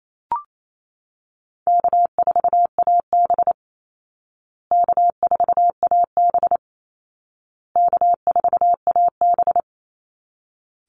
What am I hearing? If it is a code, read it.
Morse code, heard as a keyed tone, about 28 wpm, 700 Hz. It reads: K4AB K4AB K4AB
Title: Easy Call Signs Repeated 3x - 28wpm